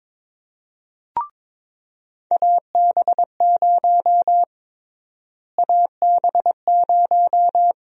Morse code, heard as a keyed tone, about 22 wpm, 700 Hz. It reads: AB0 AB0
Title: USA Call Sign Prefixes - Repeated 2x - 22wpm